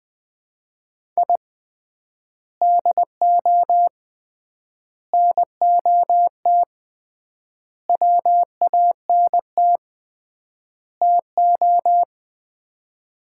Morse code, heard as a keyed tone, about 20 wpm, 700 Hz. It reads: I DO NOT WANT TO
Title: 3x Word-Spacing - Sets of 5 Words - 20wpm